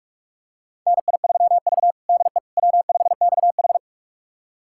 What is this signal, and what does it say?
Morse code, heard as a keyed tone, about 45 wpm, 700 Hz. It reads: NI3V DE W5XH
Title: USA Callsign DE Callsign - 4 Characters - 45wpm